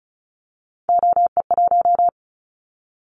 Morse code, heard as a keyed tone, about 36 wpm, 700 Hz. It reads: OE1
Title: Callsign Prefixes for Top 50 DXCC - 35wpm